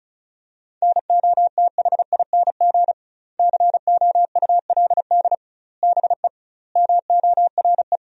Morse code, heard as a keyed tone, about 34 wpm, 700 Hz. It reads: NOTHING COULD BE MORE
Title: 2x Word-Spacing - Sentences from Top 500 Words - 35wpm